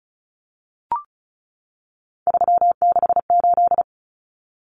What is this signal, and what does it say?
Morse code, heard as a keyed tone, about 34 wpm, 700 Hz. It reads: 368
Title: Numbers - 3-Digit 35wpm